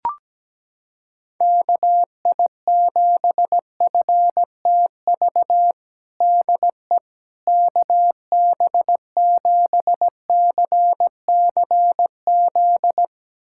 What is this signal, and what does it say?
Morse code, heard as a keyed tone, about 17 wpm, 700 Hz. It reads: KI7FTV DE KB7CCZ
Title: USA Callsign DE Callsign - 6 Characters - 17wpm